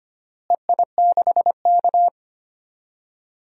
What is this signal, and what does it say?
Morse code, heard as a keyed tone, about 25 wpm, 700 Hz. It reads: EI6K
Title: DXCC-100 Call Signs - 4 Characters - Repeated 2x - 25wpm